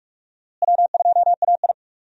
Morse code, heard as a keyed tone, about 46 wpm, 700 Hz. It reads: W2AI
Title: USA Call Signs - 4 Characters - Repeated 3x - 45wpm